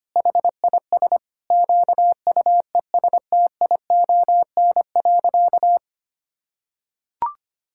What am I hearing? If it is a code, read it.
Morse code, heard as a keyed tone, about 25 wpm, 700 Hz. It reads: HIS QUESTION.
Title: Sentences from Top 1000 Words - Review Part 6 - 25wpm